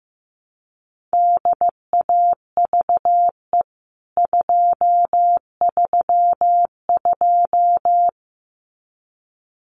Morse code, heard as a keyed tone, about 15 wpm, 700 Hz. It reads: DAVE 232
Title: CWT Exchange 15wpm